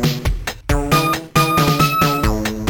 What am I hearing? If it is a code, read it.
There is no code.